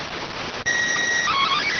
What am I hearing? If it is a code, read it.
SSTV (slow-scan television), PD 290.